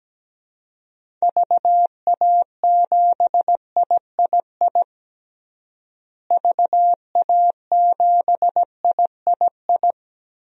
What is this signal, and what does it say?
Morse code, heard as a keyed tone, about 17 wpm, 700 Hz. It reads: VA7III VA7III
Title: DXCC-100 Call Signs - 6 Characters - Repeated 3x - 17wpm